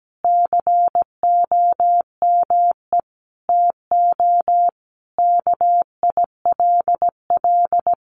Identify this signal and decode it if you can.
Morse code, heard as a keyed tone, about 17 wpm, 700 Hz. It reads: COME TO KILL
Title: Sentences from Top 1400 Words 17wpm